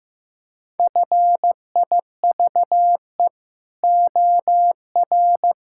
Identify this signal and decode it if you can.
Morse code, heard as a keyed tone, about 15 wpm, 700 Hz. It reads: FIVE OR
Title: Pangram Challenge v2 - 15wpm